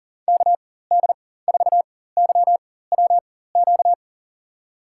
Morse code, heard as a keyed tone, about 40 wpm, spaced as for 19 wpm, 700 Hz. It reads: KD4YWQ
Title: USA Call Signs - 6 Characters - 4x Character Spacing - 40wpm